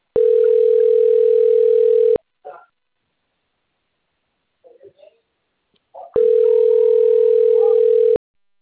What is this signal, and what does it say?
Ringback tone (North America), 440 + 480 Hz.